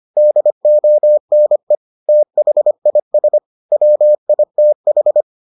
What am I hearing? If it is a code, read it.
Morse code, heard as a keyed tone, about 25 wpm, 600 Hz. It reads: DONE THIS WITH